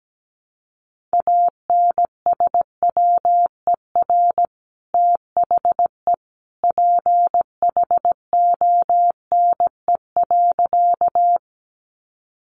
Morse code, heard as a keyed tone, about 17 wpm, 700 Hz. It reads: ANSWER THE PHONE.
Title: Sentences from Top 1100 Words 17wpm